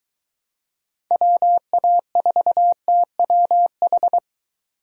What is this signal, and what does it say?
Morse code, heard as a keyed tone, about 23 wpm, 700 Hz. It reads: WA4TWH